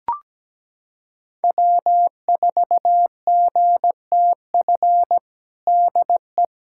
Morse code, heard as a keyed tone, about 17 wpm, 700 Hz. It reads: W4GTF DE
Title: USA Callsign DE Callsign 17wpm